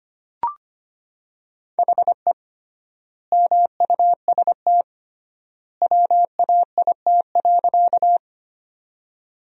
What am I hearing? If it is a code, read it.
Morse code, heard as a keyed tone, about 25 wpm, 700 Hz. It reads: HE MUST WAIT.
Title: 3x Word-Spacing - Sentences from Top 500 Words - 25wpm